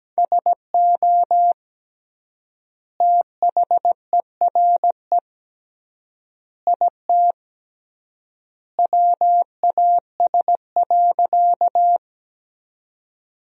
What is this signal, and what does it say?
Morse code, heard as a keyed tone, about 17 wpm, 700 Hz. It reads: SO THERE IT WAS.
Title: 3x Word-Spacing - Sentences from Top 100 Words - 17wpm